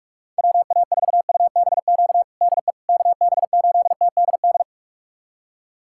Morse code, heard as a keyed tone, about 46 wpm, 700 Hz. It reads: WA4UBQ DE KB8TDD
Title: USA Callsign DE Callsign - 6 Characters - 45wpm